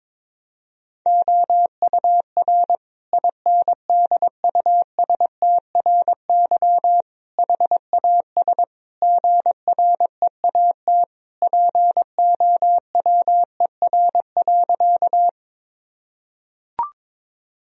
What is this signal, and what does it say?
Morse code, heard as a keyed tone, about 22 wpm, 700 Hz. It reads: OUR INDUSTRY HAS GREAT POWER.
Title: Sentences from Top 1000 Words - Review Part 4 - 22wpm